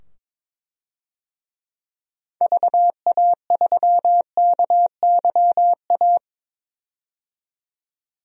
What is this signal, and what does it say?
Morse code, heard as a keyed tone, about 22 wpm, 700 Hz. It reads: VA3KYA
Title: DXCC-100 Call Signs - 6 Characters - 22wpm